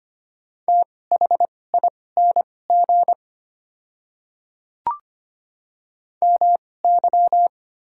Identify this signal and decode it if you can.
Morse code, heard as a keyed tone, about 25 wpm, spaced as for 18 wpm, 700 Hz. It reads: THING MY
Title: 2x Character Spacing - Top 100 Words - 25wpm